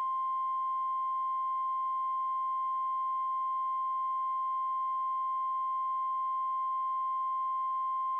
Emergency Alert System (SAME) tones (the 1050 Hz attention tone). It is heard from 0.7 to 7.7 s.
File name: NOAA Weather Radio MKE-KEC60 Weekly Test